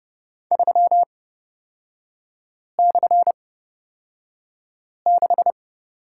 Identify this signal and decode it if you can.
Morse code, heard as a keyed tone, about 31 wpm, spaced as for 5 wpm, 690 Hz. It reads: 3/6
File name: course / 30:10wpm / 236 Lesson 38e - Letter q - Three Characters 30wpm